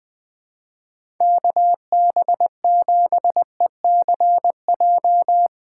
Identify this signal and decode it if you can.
Morse code, heard as a keyed tone, about 20 wpm, 700 Hz. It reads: KB7ECJ
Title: USA Call Signs - 6 Characters - 3x Character Spacing - 20wpm